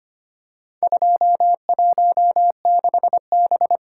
Morse code, heard as a keyed tone, about 25 wpm, 700 Hz. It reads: 216B
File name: practice / 25wpm / Four Letter-Number Combinations - 25wpm